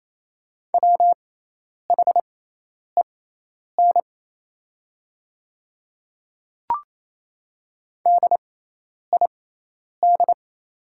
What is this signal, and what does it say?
Morse code, heard as a keyed tone, about 27 wpm, spaced as for 10 wpm, 700 Hz. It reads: WHEN DID
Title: 6x Character Spacing - Top 100 Words - 28wpm